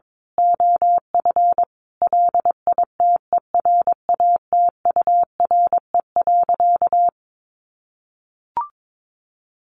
Morse code, heard as a keyed tone, about 22 wpm, 700 Hz. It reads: OF LITERATURE.
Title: Sentences from Top 900 Words 22wpm